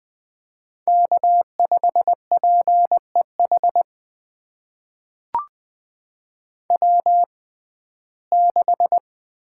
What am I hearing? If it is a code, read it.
Morse code, heard as a keyed tone, about 20 wpm, spaced as for 14 wpm, 700 Hz. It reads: K5PEH W6
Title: USA Call Signs - 5 Characters - 6x Character Spacing - 20wpm